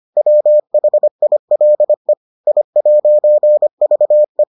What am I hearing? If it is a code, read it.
Morse code, heard as a keyed tone, about 25 wpm, 600 Hz. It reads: WHILE I'VE